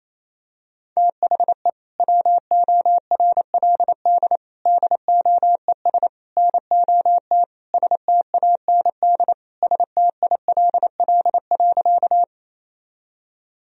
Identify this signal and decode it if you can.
Morse code, heard as a keyed tone, about 28 wpm, 700 Hz. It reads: THE WORLD DOES NOT STAND STILL.
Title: Sentences from Top 1000 Words - Review Part 1 - 28wpm